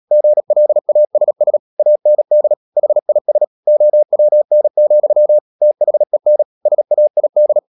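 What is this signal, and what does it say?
Morse code, heard as a keyed tone, about 37 wpm, 600 Hz. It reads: GLASS AND HIS OWN, THEN SAID
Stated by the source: Morse code practice at 35–39 wpm sending English text of Ken Follett's novel Never, from CW med Gnister.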